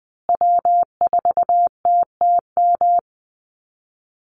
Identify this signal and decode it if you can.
Morse code, heard as a keyed tone, about 20 wpm, 700 Hz. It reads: W4TTM
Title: Call Signs Repeated 3x - 20wpm